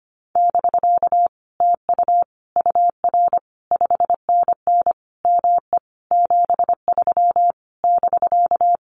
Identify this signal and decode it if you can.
Morse code, heard as a keyed tone, about 25 wpm, 700 Hz. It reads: <BK> TU UR 5NN ME 73 <BK>
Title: Common QSO Phrases Encore 25wpm